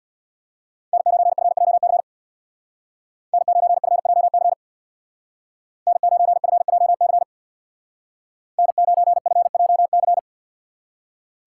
Morse code, heard as a keyed tone, about 64 wpm, 700 Hz. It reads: N9FJC N9FJC N9FJC N9FJC